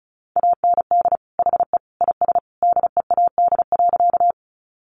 Morse code, heard as a keyed tone, about 34 wpm, 700 Hz. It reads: AND HE IS DEAD.
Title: Sentences from Top 1500 Words 35wpm